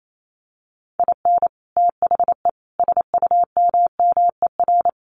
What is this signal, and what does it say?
Morse code, heard as a keyed tone, about 28 wpm, 700 Hz. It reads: IN THE SUMMER
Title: Sentences from Top 1100 Words 28wpm